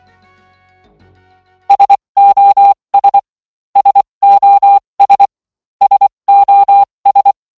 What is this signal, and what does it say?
Morse code, heard as a keyed tone, about 24 wpm, spaced as for 20 wpm, 800 Hz. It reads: SOS SOS SOS